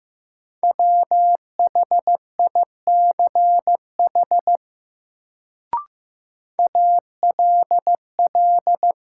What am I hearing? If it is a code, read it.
Morse code, heard as a keyed tone, about 15 wpm, 700 Hz. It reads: WHICH ALL
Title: Sentences from Top 100 Words - New Word Drill - 15wpm